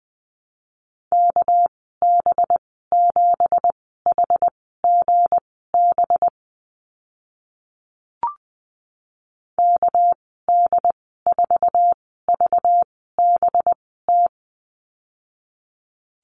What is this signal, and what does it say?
Morse code, heard as a keyed tone, about 20 wpm, spaced as for 14 wpm, 700 Hz. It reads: KB7HGB KD4VBT
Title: 245 Lesson 39g - Letter z - Call Signs 20wpm